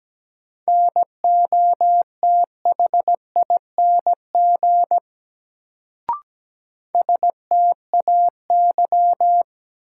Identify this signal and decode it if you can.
Morse code, heard as a keyed tone, about 17 wpm, 700 Hz. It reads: NOTHING STAY
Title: Sentences from Top 500 Words - New Word Drill - 17wpm